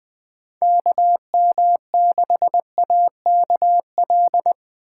Morse code, heard as a keyed tone, about 20 wpm, 700 Hz. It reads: KM6AKL